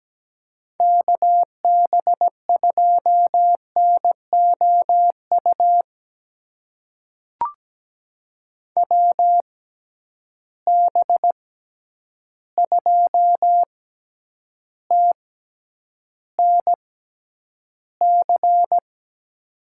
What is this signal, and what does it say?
Morse code, heard as a keyed tone, about 17 wpm, spaced as for 9 wpm, 700 Hz. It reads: KB2NOU WB2TNC
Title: USA Call Signs - 6 Characters - 6x Character Spacing - 17wpm